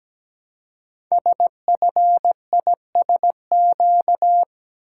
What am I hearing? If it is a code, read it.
Morse code, heard as a keyed tone, about 17 wpm, 700 Hz. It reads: SFISQ
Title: Code Groups - 5 Letter:Numbers - 17wpm